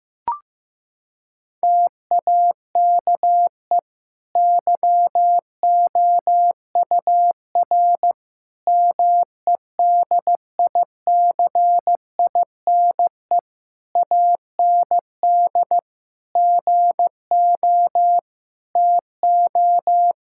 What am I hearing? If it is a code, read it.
Morse code, heard as a keyed tone, about 15 wpm, 700 Hz. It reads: TAKE YOUR MEDICINE AND GO TO
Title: Sentences from Top 1000 Words - Review Part 6 - 15wpm